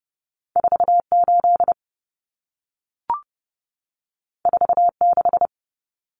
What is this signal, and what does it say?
Morse code, heard as a keyed tone, about 30 wpm, 700 Hz. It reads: 48 46
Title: Common Cut Numbers - 2-Digit 30wpm